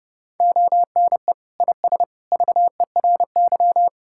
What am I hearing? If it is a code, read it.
Morse code, heard as a keyed tone, about 31 wpm, 700 Hz. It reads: ONE IS VERY
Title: Sentences from Top 1000 Words - Review Part 5 - 30wpm